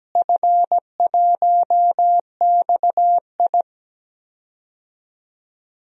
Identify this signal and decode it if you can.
Morse code, heard as a keyed tone, about 17 wpm, 700 Hz. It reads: F1XI
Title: DXCC-100 Call Signs - 4 Characters - Repeated 3x - 17wpm